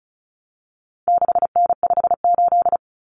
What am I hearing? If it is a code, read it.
Morse code, heard as a keyed tone, about 34 wpm, 700 Hz. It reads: 6N58
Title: Common Cut Numbers - 4-Digit 35wpm